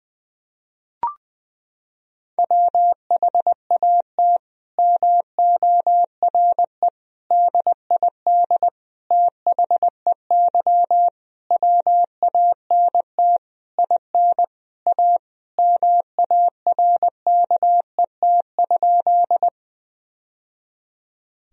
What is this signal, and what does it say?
Morse code, heard as a keyed tone, about 21 wpm, 700 Hz. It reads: WHAT MORE DID THEY WANT IN A MARKET?
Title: Sentences from Top 1000 Words - Review Part 3 - 20wpm